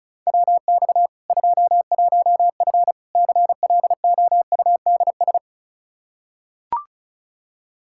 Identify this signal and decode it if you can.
Morse code, heard as a keyed tone, about 36 wpm, 700 Hz. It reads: WX 21F CLOUDS
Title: K1USN SST QSO Practice 35wpm